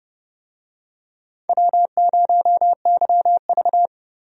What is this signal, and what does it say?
Morse code, heard as a keyed tone, about 31 wpm, 700 Hz. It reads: W0YV
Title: USA Call Signs - 4 Characters - Repeated 2x - 30wpm